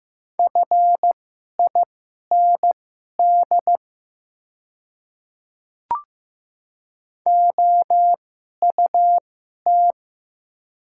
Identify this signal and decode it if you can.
Morse code, heard as a keyed tone, about 15 wpm, spaced as for 11 wpm, 690 Hz. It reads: FIND OUT
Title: 2x Character Spacing - Top 100 Words - 15wpm